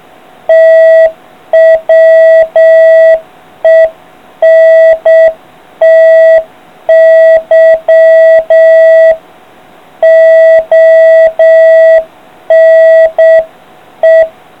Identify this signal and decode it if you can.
Morse code, heard as a keyed tone, about 7 wpm, 650 Hz. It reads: TWENTY ONE